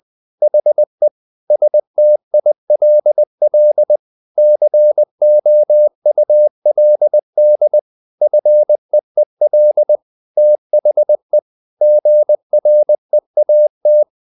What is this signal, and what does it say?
Morse code, heard as a keyed tone, about 20 wpm, 600 Hz. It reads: HE STILL COULD FEEL THE GREAT
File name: the old man and the sea / the old man and the sea part 1 20wpm